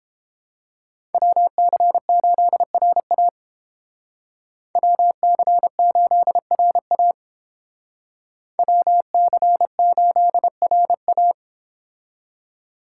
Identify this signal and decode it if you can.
Morse code, heard as a keyed tone, about 29 wpm, 700 Hz. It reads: WC8RA WC8RA WC8RA